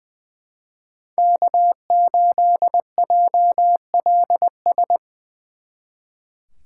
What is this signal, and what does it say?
Morse code, heard as a keyed tone, about 24 wpm, 700 Hz. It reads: K8JLS
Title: Speed Racing - USA Call Signs 20wpm